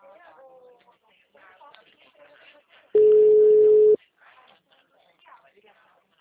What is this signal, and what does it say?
Ringback tone (Europe, 425 Hz), with speech.
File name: out- -2002- - - .60793